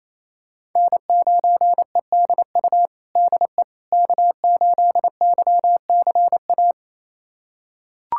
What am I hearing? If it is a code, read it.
Morse code, heard as a keyed tone, about 28 wpm, 700 Hz. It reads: N9EDU DE K8YCA
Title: USA Callsign DE Callsign - 5 Characters - 28wpm